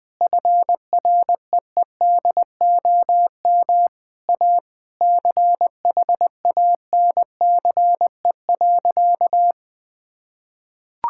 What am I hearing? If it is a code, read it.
Morse code, heard as a keyed tone, about 21 wpm, 700 Hz. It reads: FREEDOM A CHANCE.
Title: Sentences from Top 1000 Words - Review Part 5 - 20wpm